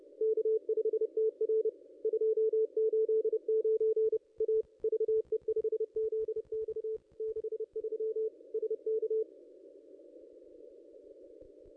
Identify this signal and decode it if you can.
Morse code, heard as a keyed tone, about 30 wpm, 440 Hz. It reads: K5TR 289 A VE5ZX 63 SK
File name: contests / ss-cw-k5tr-op-n5rz -2200